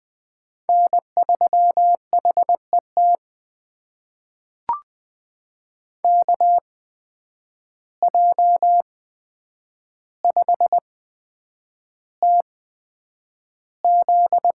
Morse code, heard as a keyed tone, about 20 wpm, spaced as for 9 wpm, 700 Hz. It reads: N3HET KJ5TZ
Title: USA Call Signs - 5 Characters - 8x Character Spacing - 20wpm